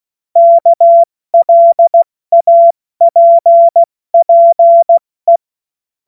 Morse code, heard as a keyed tone, about 16 wpm, 680 Hz. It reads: KLAPPE
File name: Morsekurs / Witze / Witz 24. August 2021, 70 F80